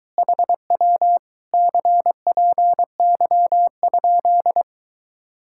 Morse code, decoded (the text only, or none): HW CPY?